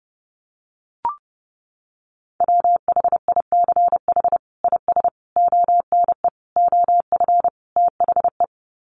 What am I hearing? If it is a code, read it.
Morse code, heard as a keyed tone, about 30 wpm, 700 Hz. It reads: WHICH IS ONE OF THE
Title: Sets of 5 Words Encore 30wpm